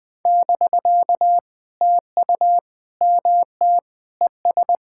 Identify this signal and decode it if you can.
Morse code, heard as a keyed tone, about 21 wpm, 700 Hz. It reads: <BK> TU MT ES
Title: POTA QSO Practice - 20wpm